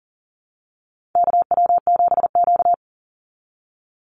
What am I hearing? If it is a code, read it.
Morse code, heard as a keyed tone, about 40 wpm, 700 Hz. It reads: KW7Q
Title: Call Signs Repeated 2x - 40wpm